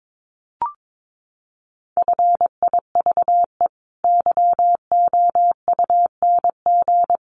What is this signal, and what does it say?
Morse code, heard as a keyed tone, about 22 wpm, 700 Hz. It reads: FIVE YOUNG
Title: Sentences from Top 400 Words 22wpm